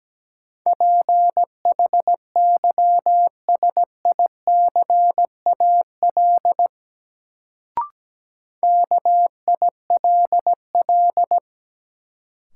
Morse code, heard as a keyed tone, about 17 wpm, 700 Hz. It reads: PHYSICAL KILL
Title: Sentences from Top 1400 Words - New Word Drill Alt - 17wpm